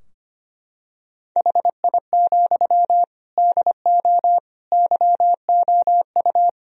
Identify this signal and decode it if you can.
Morse code, heard as a keyed tone, about 25 wpm, 700 Hz. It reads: HI, DO YOU
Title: Sentences from Top 1000 Words - Review Part 3 - 25wpm